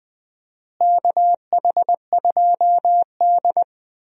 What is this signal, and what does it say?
Morse code, heard as a keyed tone, about 20 wpm, 700 Hz. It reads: KH2D
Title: USA Call Signs - 4 Characters - Repeated 2x - 20wpm